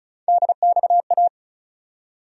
Morse code, heard as a keyed tone, about 36 wpm, 700 Hz. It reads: DXA
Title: USA Call Sign Postfixes - 3 Letters 35wpm